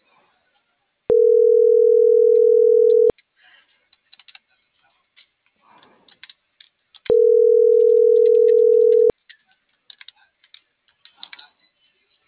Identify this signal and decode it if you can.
Ringback tone (North America), 440 + 480 Hz.